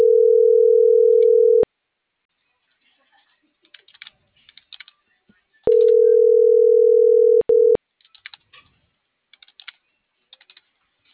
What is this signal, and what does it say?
Ringback tone (North America), 440 + 480 Hz.
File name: CENTRAL NMP / out melanie.diaznmp